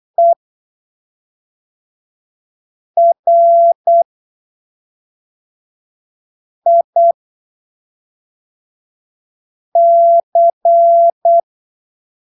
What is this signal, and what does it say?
Morse code, heard as a keyed tone, about 8 wpm, spaced as for 3 wpm, 680 Hz. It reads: ERIC
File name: arquivos / texto / texto 10 08ppm